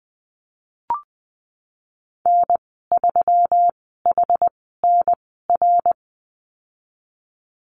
Morse code, heard as a keyed tone, about 20 wpm, spaced as for 15 wpm, 700 Hz. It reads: N3HNR
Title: 173 Lesson 28g - Number 9 - Call Signs 20wpm